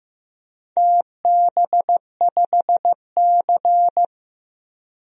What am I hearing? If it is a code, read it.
Morse code, heard as a keyed tone, about 15 wpm, 700 Hz. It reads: TB5C
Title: DXCC-100 Call Signs - 4 Characters - 15wpm